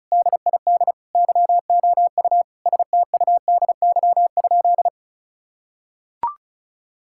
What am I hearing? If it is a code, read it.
Morse code, heard as a keyed tone, about 36 wpm, 700 Hz. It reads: DID YOU STUDY?
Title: Sentences from Top 1000 Words - Review Part 6 - 35wpm